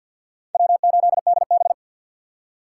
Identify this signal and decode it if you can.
Morse code, heard as a keyed tone, about 46 wpm, 700 Hz. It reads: W8DB
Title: USA Call Signs - 4x Character Spacing - 50wpm